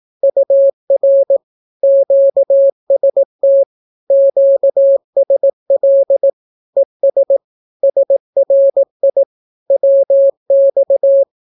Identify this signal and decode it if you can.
Morse code, heard as a keyed tone, about 18 wpm, 550 Hz. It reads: UR QST QSL ES SRI WX
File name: naukapodstaw / skroty / skroty